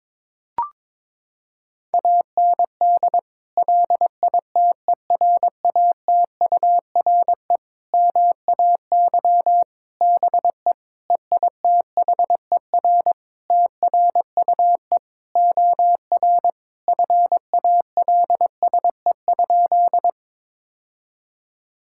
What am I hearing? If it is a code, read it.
Morse code, heard as a keyed tone, about 22 wpm, 700 Hz. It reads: AND LITERATURE MAY BE EITHER TRUE OR FALSE?
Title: Sentences from Top 1600 Words 22wpm